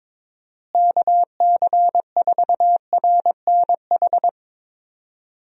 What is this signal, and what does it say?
Morse code, heard as a keyed tone, about 22 wpm, 700 Hz. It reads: KC4RNH
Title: USA Call Signs - 6 Characters - Repeated 3x - 22wpm